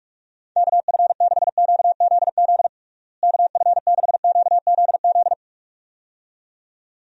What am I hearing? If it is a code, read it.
Morse code, heard as a keyed tone, about 46 wpm, 700 Hz. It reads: KF6QZZ KF6QZZ
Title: USA Call Signs - 6 Characters - Repeated 2x - 45wpm